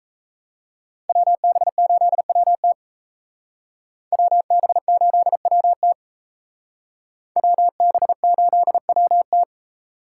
Morse code, heard as a keyed tone, about 35 wpm, 700 Hz. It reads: WB8WT WB8WT WB8WT
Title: Speed Racing - USA 2x2 Call Signs 28wpm